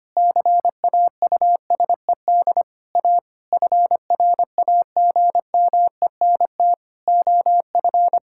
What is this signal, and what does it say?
Morse code, heard as a keyed tone, about 25 wpm, 700 Hz. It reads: CAUSED A FRAGMENT OF
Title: Pangram Challenge v2 - 25wpm